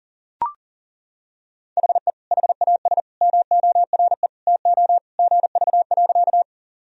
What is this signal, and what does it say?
Morse code, heard as a keyed tone, about 39 wpm, 700 Hz. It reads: HE HAS MORE TO GV.
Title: Sentences from Top 200 Words and Abbreviations 40wpm